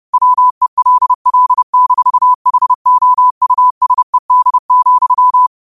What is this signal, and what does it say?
Morse code, heard as a keyed tone, about 30 wpm, 1000 Hz. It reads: WELL-HOUSED,